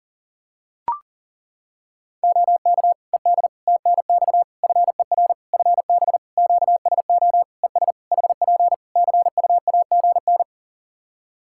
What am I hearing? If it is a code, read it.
Morse code, heard as a keyed tone, about 40 wpm, 700 Hz. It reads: OK ED TNX FER FB QSO ES HP CUAGN
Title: K1USN SST QSO Practice 40wpm